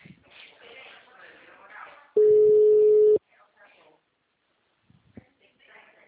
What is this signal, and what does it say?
Ringback tone (Europe, 425 Hz), with speech.
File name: out- -2026- - - .40481